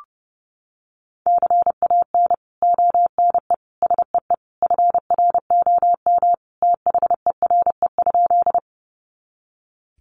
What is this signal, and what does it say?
Morse code, heard as a keyed tone, about 30 wpm, 700 Hz. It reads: CAN ONE SEE FROM THERE?
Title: Sentences from Top 100 Words 30wpm